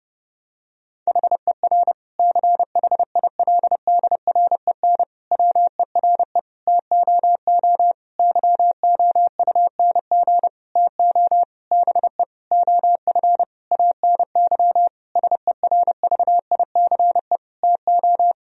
Morse code, heard as a keyed tone, about 30 wpm, 700 Hz. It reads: HER CHILDREN WERE TOO YOUNG TO BE OF ANY SERVICE TO
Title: Sentences from Top 1000 Words - Review Part 4 - 30wpm